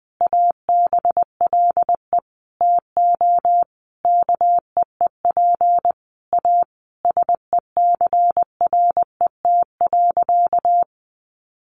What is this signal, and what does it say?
Morse code, heard as a keyed tone, about 20 wpm, 700 Hz. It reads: ABLE TO KEEP A SECRET.
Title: Sentences from Top 1300 Words 20wpm